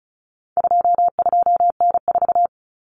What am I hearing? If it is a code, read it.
Morse code, heard as a keyed tone, about 34 wpm, 700 Hz. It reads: 22N4
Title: Common Cut Numbers - 4-Digit 35wpm